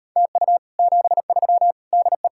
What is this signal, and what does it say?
Morse code, heard as a keyed tone, about 38 wpm, 700 Hz. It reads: TU 73 DE